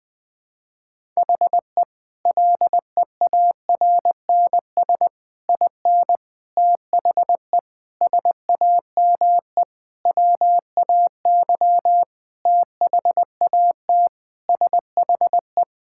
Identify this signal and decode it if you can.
Morse code, heard as a keyed tone, about 20 wpm, 700 Hz. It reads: HE LEARNS IN THE SAME WAY THAT SHE
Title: Sentences with 5-Word Phrases - Spoken Beginning - 20wpm